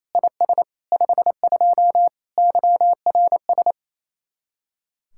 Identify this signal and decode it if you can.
Morse code, heard as a keyed tone, about 37 wpm, 700 Hz. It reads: IS 52 YRS